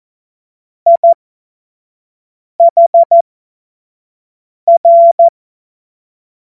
Morse code, heard as a keyed tone, about 14 wpm, spaced as for 5 wpm, 680 Hz. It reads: IHR
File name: Morsekurs / Witze / Witz 08. November 2022 25 F70